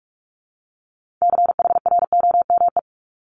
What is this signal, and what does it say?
Morse code, heard as a keyed tone, about 46 wpm, 700 Hz. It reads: CHROME
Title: Multi-letter Phonemes Repeated 3x 45wpm